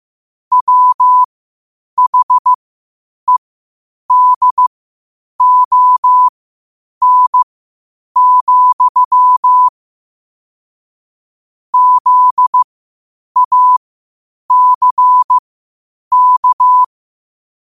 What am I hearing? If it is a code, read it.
Morse code, heard as a keyed tone, about 15 wpm, spaced as for 8 wpm, 1000 Hz. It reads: WHEDON, ZACK
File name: qotdfiles / 8wpm